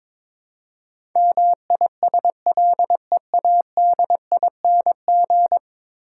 Morse code, heard as a keyed tone, about 22 wpm, 700 Hz. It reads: MISLEADING
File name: practice / 22wpm / 10 Letter Words - First 3 Letters Given - 22wpm